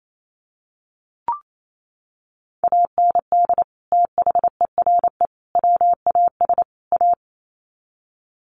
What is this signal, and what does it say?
Morse code, heard as a keyed tone, about 28 wpm, 700 Hz. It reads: AND THERE WAS A
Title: Sets of 4 Words 28wpm